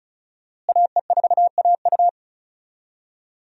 Morse code, heard as a keyed tone, about 34 wpm, 700 Hz. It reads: AE4AU
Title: USA Call Signs - 5 Characters - Repeated 2x - 35wpm